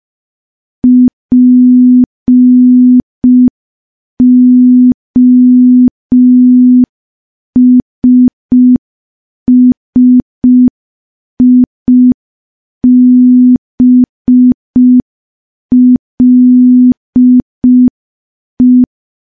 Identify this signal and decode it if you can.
Morse code, heard as a keyed tone, about 5 wpm, 250 Hz. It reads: POSSIBLE
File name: morse / morse-training-05WPM